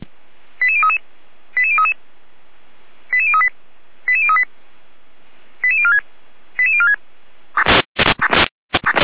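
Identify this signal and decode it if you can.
Five-tone selective call (ZVEI), heard as quick runs of five tones, tones of about 70 ms. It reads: 88033 88033 88038 88038 88046 88046
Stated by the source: ZVEI five-tone selective call received by SDR.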